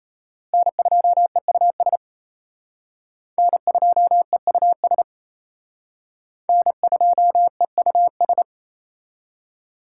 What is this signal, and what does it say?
Morse code, heard as a keyed tone, about 32 wpm, 700 Hz. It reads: N2EUS N2EUS N2EUS
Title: Speed Racing - USA 1x3 Call Signs 28wpm